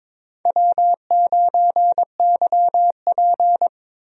Morse code, heard as a keyed tone, about 21 wpm, 700 Hz. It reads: W9YP
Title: USA Call Signs - 4 Characters - Repeated 3x - 22wpm